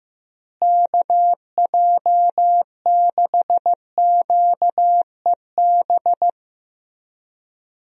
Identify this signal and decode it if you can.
Morse code, heard as a keyed tone, about 15 wpm, 700 Hz. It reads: KJ6QEB
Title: USA Call Signs - 6 Characters - Repeated 2x - 15wpm